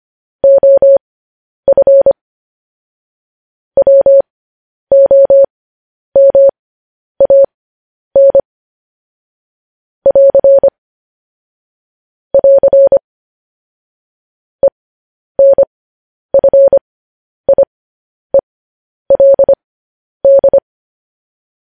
Morse code, heard as a keyed tone, about 25 wpm, spaced as for 10 wpm, 550 Hz. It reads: OF WOMAN <AR> <AR> ENFIELD
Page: News Headlines in Morse Code at 10 WPM